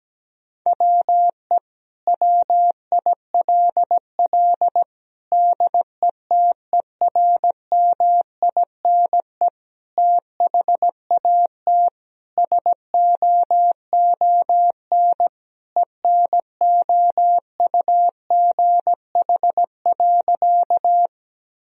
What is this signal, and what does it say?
Morse code, heard as a keyed tone, about 17 wpm, 700 Hz. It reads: WE WILL DETERMINE THAT SOON ENOUGH.
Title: Sentences from Top 1000 Words - Review Part 4 - 17wpm